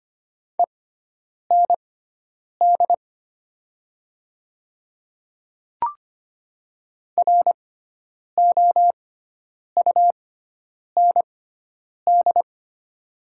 Morse code, heard as a keyed tone, about 25 wpm, spaced as for 9 wpm, 700 Hz. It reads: END ROUND
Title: 6x Character Spacing - Top 200 Words - 25wpm